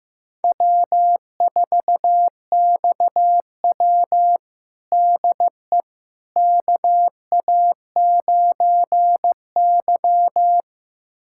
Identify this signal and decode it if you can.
Morse code, heard as a keyed tone, about 15 wpm, 700 Hz. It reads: W4XW DE KA9Y
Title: USA Callsign DE Callsign - 4 Characters - 15wpm